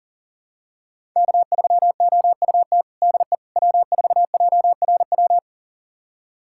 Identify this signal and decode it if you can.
Morse code, heard as a keyed tone, about 40 wpm, 700 Hz. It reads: K3OUT DE W4JRW
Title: USA Callsign DE Callsign - 5 Characters - 40wpm